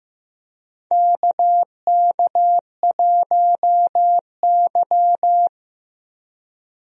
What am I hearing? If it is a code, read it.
Morse code, heard as a keyed tone, about 16 wpm, 700 Hz. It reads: KK1Y
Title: USA Call Signs - 4 Characters - Repeated 2x - 15wpm